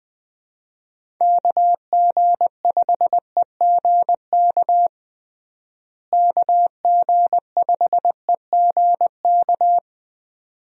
Morse code, heard as a keyed tone, about 20 wpm, 700 Hz. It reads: KG5EGK KG5EGK
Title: USA Call Signs - 6 Characters - Repeated 2x - 20wpm